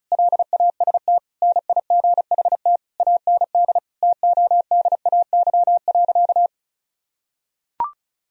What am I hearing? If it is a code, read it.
Morse code, heard as a keyed tone, about 34 wpm, 700 Hz. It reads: LAST NIGHT AND TODAY.
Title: Sentences from Top 1000 Words - Review Part 4 - 35wpm